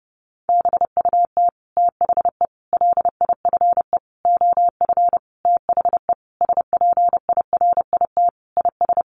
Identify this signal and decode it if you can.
Morse code, heard as a keyed tone, about 30 wpm, 700 Hz. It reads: BUT THE LIFE OF THE SPIRIT IS
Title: Sentences from Top 1500 Words 30wpm